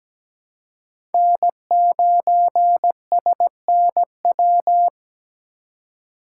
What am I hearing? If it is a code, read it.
Morse code, heard as a keyed tone, about 17 wpm, 700 Hz. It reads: N9SNW
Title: USA Call Signs - 5 Characters - 2x Character Spacing - 17wpm